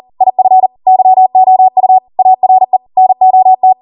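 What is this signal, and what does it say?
Morse code, heard as a keyed tone, about 40 wpm, 750 Hz. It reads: IF YOU ARE NOT
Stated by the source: Code practice at 40 wpm.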